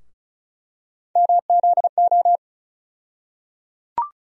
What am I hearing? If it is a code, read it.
Morse code, heard as a keyed tone, about 36 wpm, 700 Hz. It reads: MZO
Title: USA Call Sign Postfixes - 3 Letters 35wpm